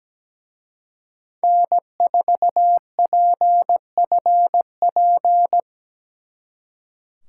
Morse code, heard as a keyed tone, about 20 wpm, 700 Hz. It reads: N4PFP
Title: Speed Racing - USA Call Signs 17wpm